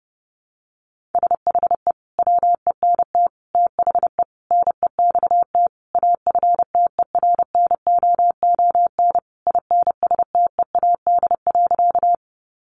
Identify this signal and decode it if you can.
Morse code, heard as a keyed tone, about 30 wpm, 700 Hz. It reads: SHE WENT THE NEXT AFTERNOON INSTEAD.
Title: Sentences from Top 1500 Words 30wpm